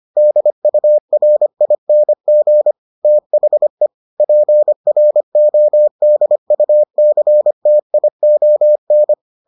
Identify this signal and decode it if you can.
Morse code, heard as a keyed tone, about 25 wpm, 600 Hz. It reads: DURING THE PRODUCTION